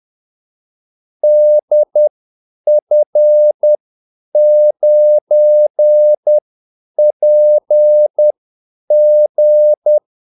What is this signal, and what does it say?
Morse code, heard as a keyed tone, about 10 wpm, spaced as for 8 wpm, 600 Hz. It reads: DF9PG
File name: Callsigns-10-8wpm